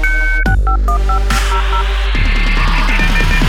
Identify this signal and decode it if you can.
DTMF, heard as telephone touch tones, tones of about 100 ms. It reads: 651500